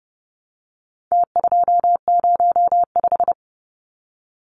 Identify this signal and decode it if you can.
Morse code, heard as a keyed tone, about 30 wpm, 700 Hz. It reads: T205
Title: Numbers - Contest Style - 4-Digit 30wpm